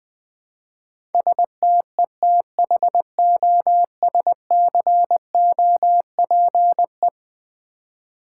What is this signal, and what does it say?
Morse code, heard as a keyed tone, about 20 wpm, 700 Hz. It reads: STETHOSCOPE
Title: Easy Words with Hints - 20wpm